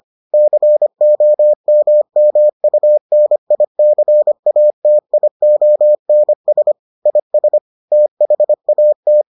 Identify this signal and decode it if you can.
Morse code, heard as a keyed tone, about 25 wpm, 600 Hz. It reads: COMMUNICATIONS IS THAT